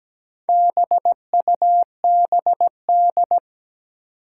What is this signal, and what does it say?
Morse code, heard as a keyed tone, about 17 wpm, 700 Hz. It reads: BUBD
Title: Four Challenging Letter-Number Combinations 17wpm